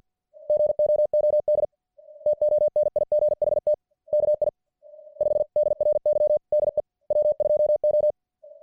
Morse code, heard as a keyed tone, about 50 wpm, spaced as for 40 wpm, 600 Hz. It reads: GOOD TONIGHT KI5DRY DE W2O